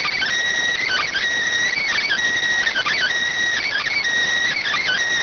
Emergency Alert System (SAME) tones (header bursts), heard as harsh buzzing bursts. It is heard from 0.7 to 5.0 s.
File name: findings / CT 205 ax mips, iDose 3 0006.gen-42 rx